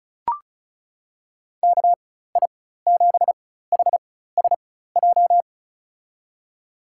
Morse code, heard as a keyed tone, about 34 wpm, spaced as for 16 wpm, 700 Hz. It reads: KI7HSJ
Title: USA Call Signs - 6 Characters - 4x Character Spacing - 35wpm